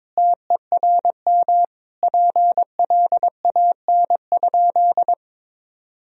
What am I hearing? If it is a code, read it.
Morse code, heard as a keyed tone, about 21 wpm, 700 Hz. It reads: TERM PLAN?
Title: Sentences from Top 1000 Words - Review Part 4 - 22wpm